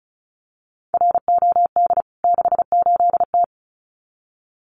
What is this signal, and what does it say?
Morse code, heard as a keyed tone, about 34 wpm, 700 Hz. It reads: ROD 68T
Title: CWT Exchange 35wpm